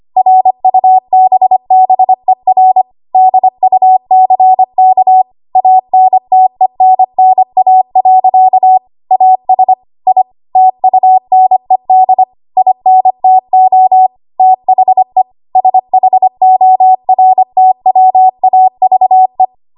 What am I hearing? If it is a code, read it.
Morse code, heard as a keyed tone, about 25 wpm, 750 Hz. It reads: RUBBER DUCK ANTENNA. AS I TUNED INTO THE SHORTWAVE
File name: Archive / 25 WPM / 25WPM